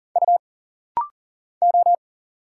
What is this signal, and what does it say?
Morse code, heard as a keyed tone, about 40 wpm, 700 Hz. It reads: U O
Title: Single Letters - Rapid-Fire 40wpm